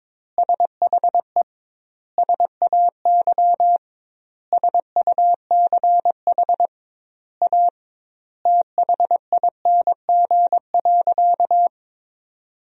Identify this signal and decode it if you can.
Morse code, heard as a keyed tone, about 22 wpm, 700 Hz. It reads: SHE SAY SUCH A THING.
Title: 2x Word-Spacing - Sentences from Top 200 Words - 22wpm